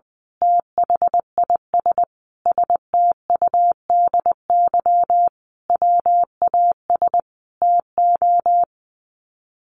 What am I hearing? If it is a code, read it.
Morse code, heard as a keyed tone, about 20 wpm, 700 Hz. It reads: THIS STUDY WAS TO
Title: Sets of 5 Words 20wpm